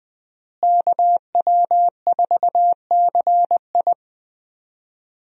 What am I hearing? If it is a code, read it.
Morse code, heard as a keyed tone, about 20 wpm, 700 Hz. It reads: KW4CI
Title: USA Call Signs - 5 Characters - Repeated 2x - 20wpm